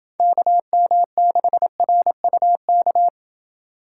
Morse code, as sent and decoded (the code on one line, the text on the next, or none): -.- -- -.... .-. ..- -.-
KM6RUK